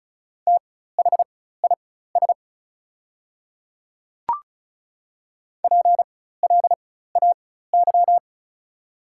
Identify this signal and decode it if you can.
Morse code, heard as a keyed tone, about 36 wpm, spaced as for 16 wpm, 690 Hz. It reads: THIS PLAY
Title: 4x Character Spacing - Top 500 Words - 35wpm